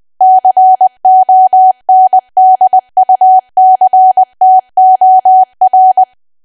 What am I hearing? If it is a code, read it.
Morse code, heard as a keyed tone, about 20 wpm, 750 Hz. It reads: CONDUCTOR
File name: cwa / W1AW-Level3 / 20WPM